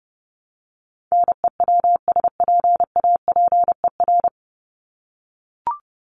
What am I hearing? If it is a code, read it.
Morse code, heard as a keyed tone, about 30 wpm, 700 Hz. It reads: NEWSPAPER
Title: Top 751-1000 Words 30wpm